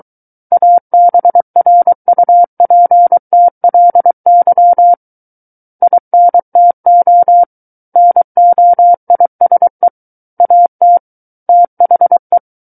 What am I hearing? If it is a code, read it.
Morse code, heard as a keyed tone, about 23 wpm, 700 Hz. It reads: ABRUPTLY INTO NOISE AT THE
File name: Book One / 14-Chapter 14 - 23 WPM